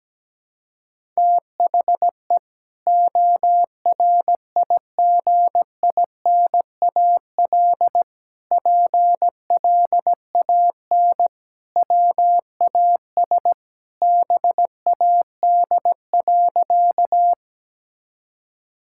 Morse code, heard as a keyed tone, about 17 wpm, 700 Hz. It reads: THE ORIGINAL PLAN WAS BAD.
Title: Sentences from Top 1000 Words - Review Part 6 - 17wpm